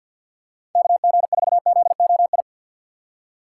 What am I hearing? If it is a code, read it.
Morse code, heard as a keyed tone, about 52 wpm, 690 Hz. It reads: KG4ZOI
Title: USA Call Signs - 6 Characters - 3x Character Spacing - 50wpm